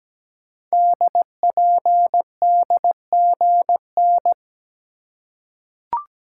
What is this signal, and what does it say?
Morse code, heard as a keyed tone, about 17 wpm, 700 Hz. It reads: DPDGN
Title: Code Groups - 5 Letters - 17wpm